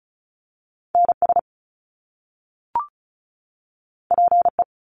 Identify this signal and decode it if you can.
Morse code, heard as a keyed tone, about 36 wpm, 700 Hz. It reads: NS PE
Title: Canadian Provinces and Territories 35wpm